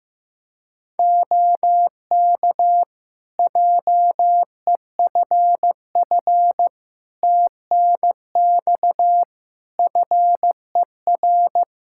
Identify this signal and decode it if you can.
Morse code, heard as a keyed tone, about 15 wpm, 700 Hz. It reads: OK JEFF TNX FER
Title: K1USN SST QSO Practice 15wpm